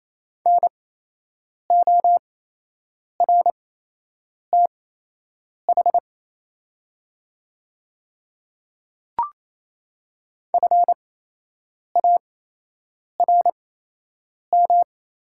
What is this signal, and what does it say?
Morse code, heard as a keyed tone, about 28 wpm, spaced as for 8 wpm, 700 Hz. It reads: NORTH FARM
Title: 8x Character Spacing - Top 500 Words - 28wpm